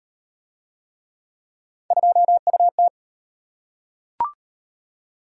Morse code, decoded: 2UT

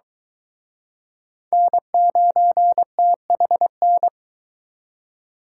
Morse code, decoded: N9THN

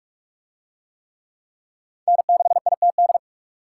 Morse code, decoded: N6ITD

